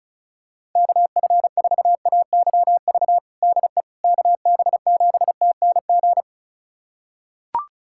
Morse code, heard as a keyed tone, about 34 wpm, 700 Hz. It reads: KF4AYV DE KB7TNG
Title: USA Callsign DE Callsign - 6 Characters - 35wpm